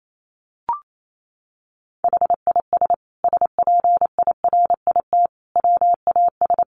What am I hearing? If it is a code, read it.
Morse code, heard as a keyed tone, about 28 wpm, 700 Hz. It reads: HIS SPIRIT WAS
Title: Sentences from Top 1400 Words 28wpm